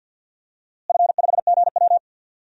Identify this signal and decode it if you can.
Morse code, heard as a keyed tone, about 50 wpm, 700 Hz. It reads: F5GW
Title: DXCC-100 Call Signs - 4 Characters - Repeated 3x - 50wpm